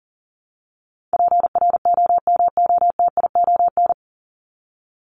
Morse code, heard as a keyed tone, about 40 wpm, 700 Hz. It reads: PROMOTION
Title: Top 1501-1750 Words 40wpm